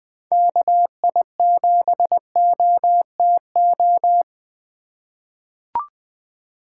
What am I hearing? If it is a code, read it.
Morse code, heard as a keyed tone, about 20 wpm, 700 Hz. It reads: KI7OTO
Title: USA Call Signs - 6 Characters - 4x Character Spacing - 20wpm